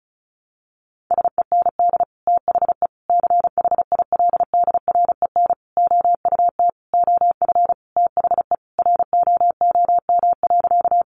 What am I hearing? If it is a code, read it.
Morse code, heard as a keyed tone, about 34 wpm, 700 Hz. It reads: SEND THE CHILDREN OUT OF THE ROOM.
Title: Sentences from Top 1500 Words 35wpm